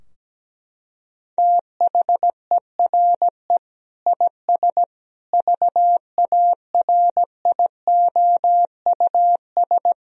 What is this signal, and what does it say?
Morse code, heard as a keyed tone, about 17 wpm, 700 Hz. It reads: THERE IS VARIOUS
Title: Sentences from Top 1000 Words - Review Part 4 - 17wpm